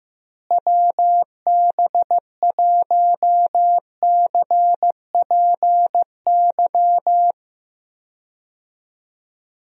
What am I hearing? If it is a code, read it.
Morse code, heard as a keyed tone, about 15 wpm, 700 Hz. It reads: WB1CPY
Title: USA Call Signs - 6 Characters - Repeated 2x - 15wpm